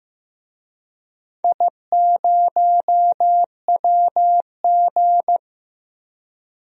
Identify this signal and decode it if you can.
Morse code, heard as a keyed tone, about 15 wpm, 700 Hz. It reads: I0WG
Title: DXCC-100 Call Signs - 4 Characters - Repeated 2x - 15wpm